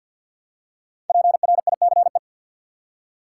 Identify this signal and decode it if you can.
Morse code, heard as a keyed tone, about 50 wpm, 700 Hz. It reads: PRICE
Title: Five Letter Words - 50wpm